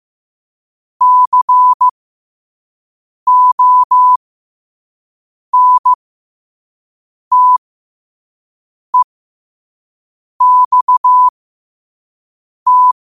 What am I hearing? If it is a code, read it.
Morse code, heard as a keyed tone, about 15 wpm, spaced as for 5 wpm, 1000 Hz. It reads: CONTEXT
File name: qotdfiles / 5wpm